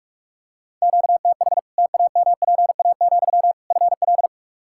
Morse code, heard as a keyed tone, about 41 wpm, 700 Hz. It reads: QTH TAMPA, FL
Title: Practice QSO - Exchange 1 - Beginning - 45wpm